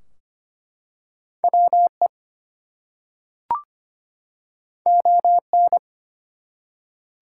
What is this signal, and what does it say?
Morse code, heard as a keyed tone, about 25 wpm, 700 Hz. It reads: WE ON
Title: Two Letter Words - 25wpm